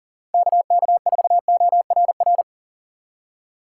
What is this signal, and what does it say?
Morse code, heard as a keyed tone, about 40 wpm, 700 Hz. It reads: KK4ORR
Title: USA Call Signs - 6 Characters - 3x Character Spacing - 40wpm